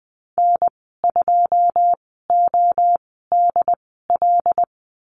Morse code, heard as a keyed tone, about 20 wpm, spaced as for 14 wpm, 700 Hz. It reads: N2ODL
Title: 082 Lesson 14g - Number 2 - Call Signs 20wpm